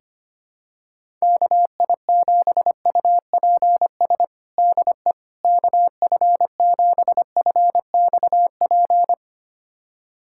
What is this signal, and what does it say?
Morse code, heard as a keyed tone, about 25 wpm, 700 Hz. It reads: KI7UPS DE KF7FXP
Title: USA Callsign DE Callsign - 6 Characters - 25wpm